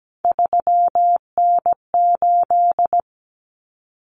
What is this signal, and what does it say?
Morse code, heard as a keyed tone, about 17 wpm, 700 Hz. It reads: 3N8
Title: Common Cut Numbers - 3-Digit 17wpm